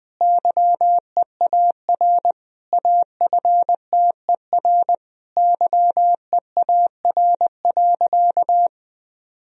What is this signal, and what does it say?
Morse code, heard as a keyed tone, about 19 wpm, 700 Hz. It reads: YEAR AFTER YEAR.
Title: Sentences from Top 1000 Words - Review Part 1 - 20wpm